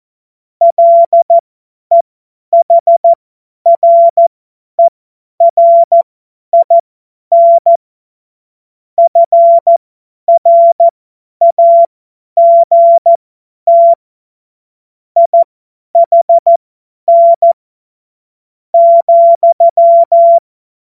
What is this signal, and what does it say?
Morse code, heard as a keyed tone, about 14 wpm, spaced as for 10 wpm, 680 Hz. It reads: LEHRERIN FRAGT IHN,
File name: Morsekurs / Witze / Witz 14. März 2023 50 F70